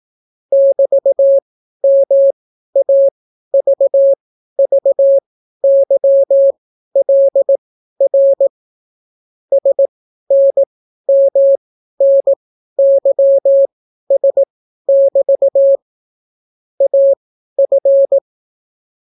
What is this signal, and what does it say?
Morse code, heard as a keyed tone, about 18 wpm, spaced as for 12 wpm, 550 Hz. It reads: <BT>MAVVYLR SNMNYS<BT> AF